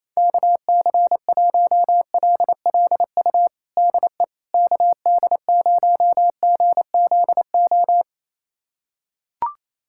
Morse code, heard as a keyed tone, about 28 wpm, 700 Hz. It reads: KC1LLU DE KD0GZO
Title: USA Callsign DE Callsign 28wpm